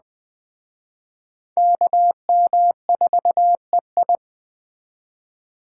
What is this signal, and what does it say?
Morse code, heard as a keyed tone, about 20 wpm, 700 Hz. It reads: KM4EI
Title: Speed Racing - USA 2x2 Call Signs 17wpm